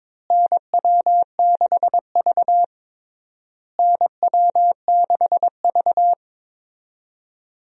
Morse code, heard as a keyed tone, about 22 wpm, 700 Hz. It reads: NW6V NW6V
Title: USA Call Signs - 4 Characters - Repeated 3x - 22wpm